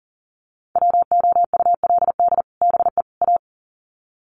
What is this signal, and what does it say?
Morse code, heard as a keyed tone, about 40 wpm, 700 Hz. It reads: WOULD BE A
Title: Sets of 3 Words 40wpm